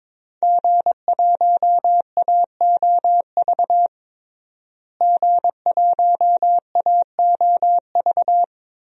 Morse code, heard as a keyed tone, about 22 wpm, 700 Hz. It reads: G1AOV G1AOV
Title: DXCC-100 Call Signs - 5 Characters - Repeated 3x - 22wpm